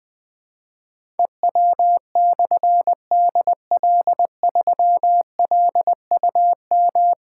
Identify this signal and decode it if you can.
Morse code, heard as a keyed tone, about 20 wpm, 700 Hz. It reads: EW/DL3LUM
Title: Difficult Call Signs 20wpm